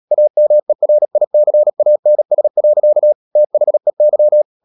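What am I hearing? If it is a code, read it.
Morse code, heard as a keyed tone, about 37 wpm, 600 Hz. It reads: AMERICANS. THEY
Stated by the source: Morse code practice at 35–39 wpm sending English text of Ken Follett's novel Never, from CW med Gnister.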